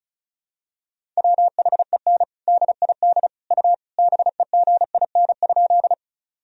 Morse code, heard as a keyed tone, about 34 wpm, 700 Hz. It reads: WHEN DID U BEGIN?
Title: Sentences from Top 300 Words and Abbreviations 35wpm